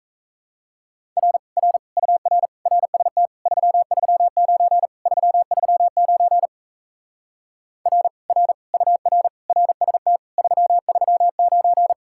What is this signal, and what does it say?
Morse code, heard as a keyed tone, about 37 wpm, 700 Hz. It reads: R R UR RST 339 339 R R UR RST 339